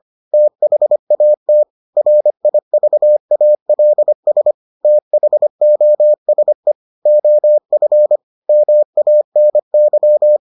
Morse code, heard as a keyed tone, about 25 wpm, 600 Hz. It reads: THAT RIVALS THOSE OF MANY